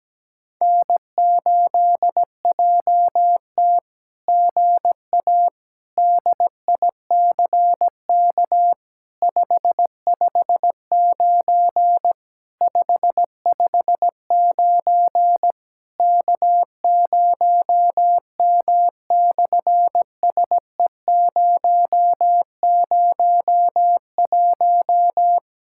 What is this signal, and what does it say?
Morse code, heard as a keyed tone, about 17 wpm, 700 Hz. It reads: N8JT GA DICK 559 559 K0M/SE001
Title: SOTA QSO Practice 17wpm